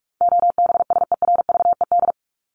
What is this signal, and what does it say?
Morse code, heard as a keyed tone, about 44 wpm, 700 Hz. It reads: OBSERVED
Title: ED Suffix - Top 100 Words with Base Words 45wpm